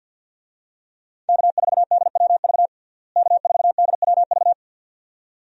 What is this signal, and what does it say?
Morse code, heard as a keyed tone, about 49 wpm, 700 Hz. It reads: K4DWV K4DWV
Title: USA Call Signs - 5 Characters - Repeated 2x - 50wpm